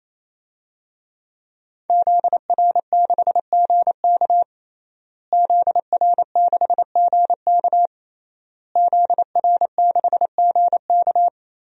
Morse code, heard as a keyed tone, about 28 wpm, 700 Hz. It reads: ZR6GK ZR6GK ZR6GK
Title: DXCC-100 Call Signs - 5 Characters - Repeated 3x - 28wpm